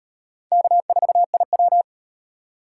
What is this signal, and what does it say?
Morse code, heard as a keyed tone, about 36 wpm, 700 Hz. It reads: K4IW